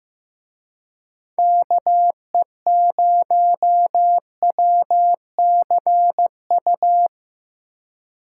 Morse code, heard as a keyed tone, about 18 wpm, 700 Hz. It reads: KE0WCU
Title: Speed Racing - USA 2x3 Call Signs 15wpm